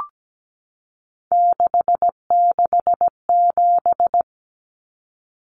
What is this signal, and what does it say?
Morse code, heard as a keyed tone, about 17 wpm, 700 Hz. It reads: 667
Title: Common Cut Numbers - 3-Digit 17wpm